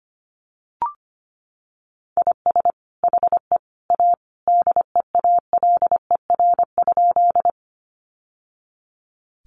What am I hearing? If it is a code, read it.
Morse code, heard as a keyed tone, about 25 wpm, 700 Hz. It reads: IS HE A DEALER?
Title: Sentences from Top 1100 Words 25wpm